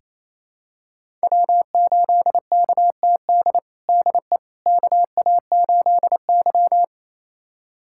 Morse code, heard as a keyed tone, about 28 wpm, 700 Hz. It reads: W8KTD DE KA8Y
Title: USA Callsign DE Callsign 28wpm